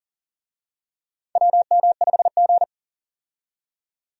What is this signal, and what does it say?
Morse code, heard as a keyed tone, about 40 wpm, 700 Hz. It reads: WM5G
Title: USA Call Signs - 4 Characters - 40wpm